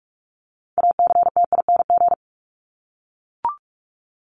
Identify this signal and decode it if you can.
Morse code, heard as a keyed tone, about 44 wpm, 700 Hz. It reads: ACTING
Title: ING Suffix - Top 100 Words with Base Words 45wpm